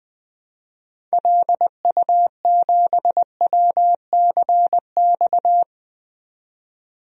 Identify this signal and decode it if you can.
Morse code, heard as a keyed tone, about 20 wpm, 700 Hz. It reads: LU7WCX
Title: DXCC-100 Call Signs - 6 Characters - 20wpm